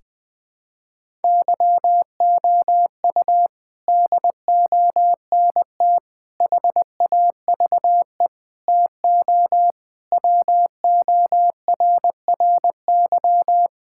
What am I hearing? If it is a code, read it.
Morse code, heard as a keyed tone, about 20 wpm, 700 Hz. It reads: YOU DONT HAVE TO WORRY
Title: Sentences with 6-Word Phrases - 20wpm